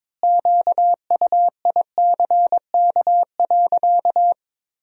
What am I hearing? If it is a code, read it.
Morse code, heard as a keyed tone, about 22 wpm, 700 Hz. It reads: QUICK.
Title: 2x Word-Spacing - Sentences from Top 500 Words - 22wpm